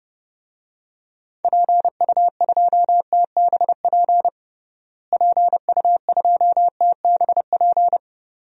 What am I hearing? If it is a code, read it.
Morse code, heard as a keyed tone, about 30 wpm, 700 Hz. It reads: PU2TBP PU2TBP
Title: DXCC-100 Call Signs - 6 Characters - Repeated 2x - 30wpm